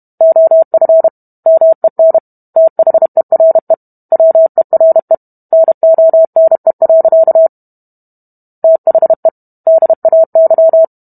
Morse code, heard as a keyed tone, about 32 wpm, 650 Hz. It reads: OF MEN THERE WERE NONE. THE DAY